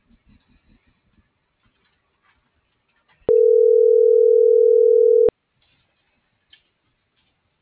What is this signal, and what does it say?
Ringback tone (North America), 440 + 480 Hz.
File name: CENTRAL MULTIMODAL / out kimberly.suarez